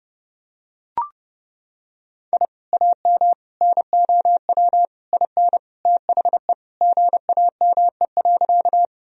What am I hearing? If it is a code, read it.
Morse code, heard as a keyed tone, about 29 wpm, 700 Hz. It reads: I AM NOW IN THE GAME.
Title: Sentences from Top 1000 Words - Review Part 3 - 30wpm